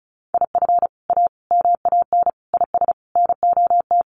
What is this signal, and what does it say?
Morse code, heard as a keyed tone, about 34 wpm, 700 Hz. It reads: IF A MAN IS NOT
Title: Sentences from Top 1400 Words 35wpm